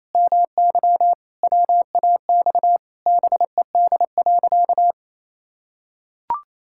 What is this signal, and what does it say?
Morse code, heard as a keyed tone, about 27 wpm, 700 Hz. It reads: MY WAX BED.
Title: Pangram Challenge v2 - 28wpm